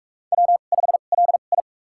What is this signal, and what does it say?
Morse code, heard as a keyed tone, about 46 wpm, spaced as for 32 wpm, 700 Hz. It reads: W5LI
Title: USA Call Signs - 4 Characters - 2x Character Spacing - 45wpm